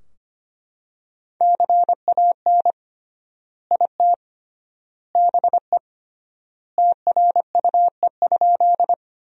Morse code, heard as a keyed tone, about 25 wpm, 700 Hz. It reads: CAN IT BE TRUE?
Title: 3x Word-Spacing - Sentences from Top 400 Words - 25wpm